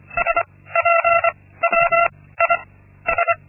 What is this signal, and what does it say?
Morse code, heard as a keyed tone, about 25 wpm, 680 Hz. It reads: S P W I S
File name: Linksmakalnis-2